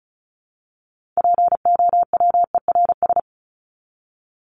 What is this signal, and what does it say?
Morse code, heard as a keyed tone, about 34 wpm, 700 Hz. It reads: POWERS
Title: ES:S Suffix - Top 100 Words with Base Words 35wpm